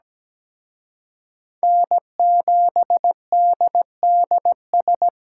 Morse code, heard as a keyed tone, about 20 wpm, 700 Hz. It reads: N7DDS
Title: Speed Racing - USA Call Signs 17wpm